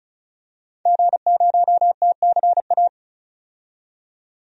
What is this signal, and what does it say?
Morse code, heard as a keyed tone, about 36 wpm, 700 Hz. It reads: G0TCA
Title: DXCC-100 Call Signs - 5 Characters - 35wpm